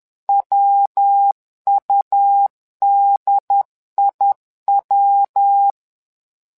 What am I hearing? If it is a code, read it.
Morse code, heard as a keyed tone, about 11 wpm, 800 Hz. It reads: WUDIW
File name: LEZIONI / 3 MWRO / 3 MWRO 10wpm 2148